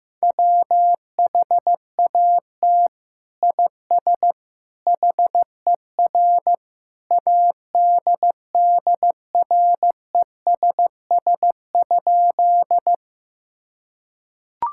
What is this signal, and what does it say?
Morse code, heard as a keyed tone, about 15 wpm, 700 Hz. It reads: WHAT IS HER ADDRESS?
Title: Sentences from Top 1000 Words - Review Part 6 - 15wpm